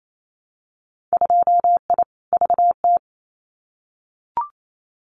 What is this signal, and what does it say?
Morse code, heard as a keyed tone, about 28 wpm, 700 Hz. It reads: 2I VT
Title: Winter Field Day Exchanges 28wpm